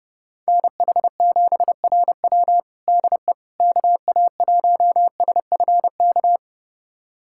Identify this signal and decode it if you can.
Morse code, heard as a keyed tone, about 30 wpm, 700 Hz. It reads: NH7RW DE KA1SFK
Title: USA Callsign DE Callsign 30wpm